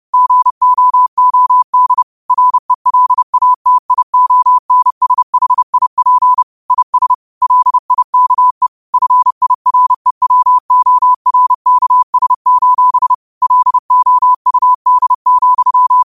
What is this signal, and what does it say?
Morse code, heard as a keyed tone, about 30 wpm, 1000 Hz. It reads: GOOD RELATIONSHIP IS LIKE FIREWORKS: LOUD,
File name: qotdfiles / 30wpm